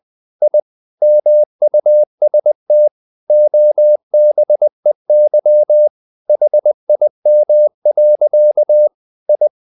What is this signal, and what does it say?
Morse code, heard as a keyed tone, about 20 wpm, 600 Hz. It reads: I MUST OBEY HIM. I
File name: the old man and the sea / the old man and the sea part 1 20wpm